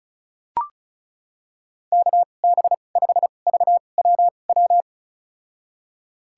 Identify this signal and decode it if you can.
Morse code, heard as a keyed tone, about 36 wpm, spaced as for 25 wpm, 700 Hz. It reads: KB5VWW
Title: USA Call Signs - 6 Characters - 2x Character Spacing - 35wpm